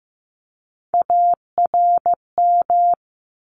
Morse code, heard as a keyed tone, about 15 wpm, 700 Hz. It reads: ARM
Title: ES:S Suffix - Top 100 Words with Base Words 15wpm